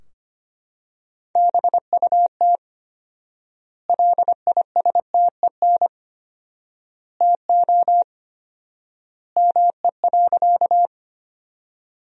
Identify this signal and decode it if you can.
Morse code, heard as a keyed tone, about 25 wpm, 700 Hz. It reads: BUT LISTEN TO ME.
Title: 4x Word-Spacing - Sentences from Top 400 Words - 25wpm